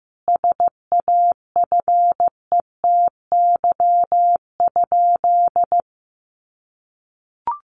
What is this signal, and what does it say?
Morse code, heard as a keyed tone, about 15 wpm, 700 Hz. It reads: SAFETY?
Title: Sentences from Top 700 Words 15wpm